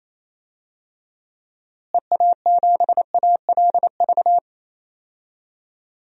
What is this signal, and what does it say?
Morse code, heard as a keyed tone, about 28 wpm, 700 Hz. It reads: EA7ALV
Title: DXCC-100 Call Signs - 6 Characters - Repeated 3x - 28wpm